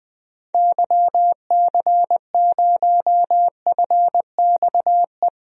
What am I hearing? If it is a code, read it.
Morse code, heard as a keyed tone, about 20 wpm, 700 Hz. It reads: YC0FXE
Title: DXCC-100 Call Signs - 6 Characters - Repeated 2x - 20wpm